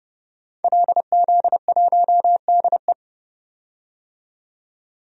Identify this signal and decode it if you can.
Morse code, heard as a keyed tone, about 30 wpm, 700 Hz. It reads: LZ1DE
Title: DXCC-100 Call Signs - 5 Characters - 30wpm